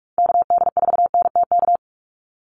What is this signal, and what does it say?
Morse code, heard as a keyed tone, about 44 wpm, 700 Hz. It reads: KD4NTK
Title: USA Call Signs 45wpm